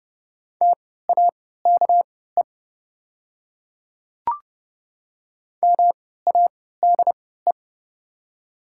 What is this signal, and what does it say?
Morse code, heard as a keyed tone, about 30 wpm, spaced as for 17 wpm, 690 Hz. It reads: TAKE MADE
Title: 3x Character Spacing - Top 200 Words - 30wpm